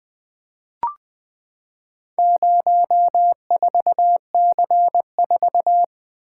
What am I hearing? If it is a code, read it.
Morse code, heard as a keyed tone, about 20 wpm, 700 Hz. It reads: 04C4
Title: Four Letter-Number Combinations - 20wpm